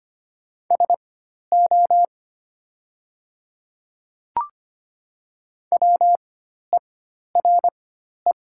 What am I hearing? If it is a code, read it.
Morse code, heard as a keyed tone, about 25 wpm, spaced as for 12 wpm, 700 Hz. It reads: SO WERE
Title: 4x Character Spacing - Top 100 Words - 25wpm